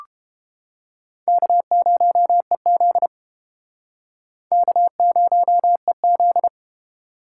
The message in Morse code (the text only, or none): K0EZ K0EZ